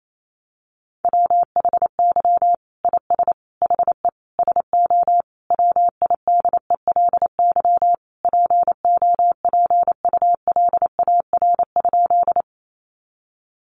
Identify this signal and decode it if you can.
Morse code, heard as a keyed tone, about 28 wpm, 700 Hz. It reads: WHY IS HE SO WIDELY POPULAR?
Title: Sentences from Top 1200 Words 28wpm